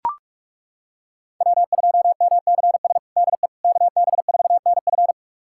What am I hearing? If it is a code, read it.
Morse code, heard as a keyed tone, about 41 wpm, 700 Hz. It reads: W2MCS DE KB4NF
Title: USA Callsign DE Callsign - 5 Characters - 45wpm